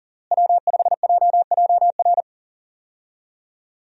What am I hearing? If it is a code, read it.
Morse code, heard as a keyed tone, about 40 wpm, 700 Hz. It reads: W5JJR
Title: USA Call Signs - 5 Characters - Repeated 3x - 40wpm